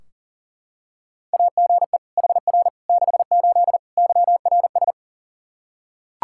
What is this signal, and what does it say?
Morse code, heard as a keyed tone, about 40 wpm, 700 Hz. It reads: AGE HR 68 YRS
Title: Practice QSO - Exchange 3 - More Info - 40wpm